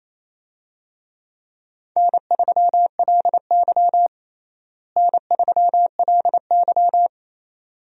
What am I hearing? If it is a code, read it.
Morse code, heard as a keyed tone, about 28 wpm, 700 Hz. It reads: N3LY N3LY
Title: USA Call Signs - 4 Characters - Repeated 2x - 28wpm